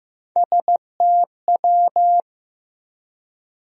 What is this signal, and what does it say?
Morse code, heard as a keyed tone, about 15 wpm, 700 Hz. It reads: STW
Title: USA Call Sign Postfixes - 3 Letters 15wpm